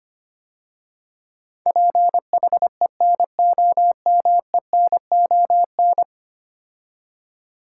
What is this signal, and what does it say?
Morse code, heard as a keyed tone, about 25 wpm, 700 Hz. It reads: PHENOMENON
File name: 10 Letter Words - First 3 Letters Given - 25wpm